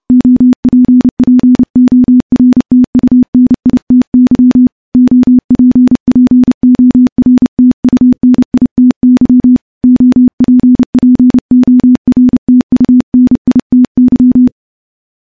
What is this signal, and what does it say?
Morse code, heard as a keyed tone, about 30 wpm, 250 Hz. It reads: OPPORTUNITY OPPORTUNITY OPPORTUNITY
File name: morse / morse-training-30WPM